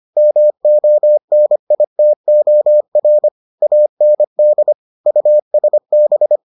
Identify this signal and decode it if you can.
Morse code, heard as a keyed tone, about 25 wpm, 600 Hz. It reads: MONITOR AND USB